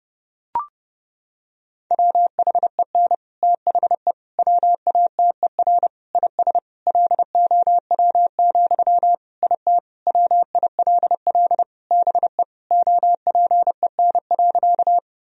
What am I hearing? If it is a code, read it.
Morse code, heard as a keyed tone, about 30 wpm, 700 Hz. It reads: WHEN THE WATER IS LOW, IT WILL BE OPEN.
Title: Sentences from Top 1000 Words - Review Part 1 - 30wpm